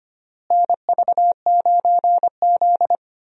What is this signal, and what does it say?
Morse code, heard as a keyed tone, about 25 wpm, 700 Hz. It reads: NV9Z
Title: USA Call Signs - 4 Characters - Repeated 2x - 25wpm